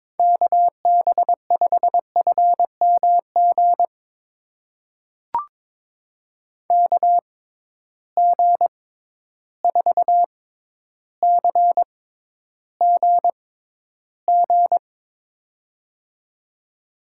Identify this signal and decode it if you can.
Morse code, heard as a keyed tone, about 21 wpm, spaced as for 11 wpm, 700 Hz. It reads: KB5FMG KG4CGG
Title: USA Call Signs - 6x Character Spacing - 22wpm